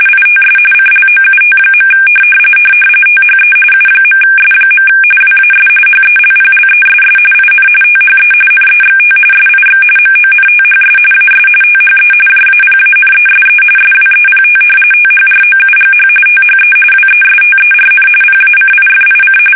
RTTY, about 45.45 baud. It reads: ITS CLOUT. THE U.S. WAS HELPLESS IN THE NEXT SIX MONTHS AS THE JAPANESE SWEPT THROUGH THE WESTERN PACIFIC
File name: fsk850